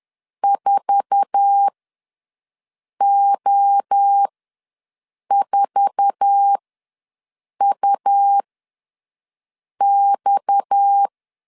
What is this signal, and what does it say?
Morse code, heard as a keyed tone, about 10 wpm, spaced as for 5 wpm, 800 Hz. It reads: 4O4UX
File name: LEZIONI / 8 YJ45 / 8 YJ4 ws 10wpm 4020